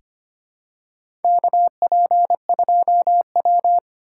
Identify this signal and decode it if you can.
Morse code, heard as a keyed tone, about 25 wpm, 700 Hz. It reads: KP2W
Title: USA Call Signs - 4x Character Spacing - 25wpm